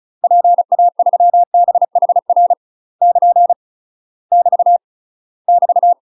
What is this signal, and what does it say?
Morse code, heard as a keyed tone, about 35 wpm, 700 Hz. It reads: PA3BHR <KN> <BT> <BT>
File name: UK MASTER QSOs TRAINING / 35WPM / UK MASTER QSOs 0015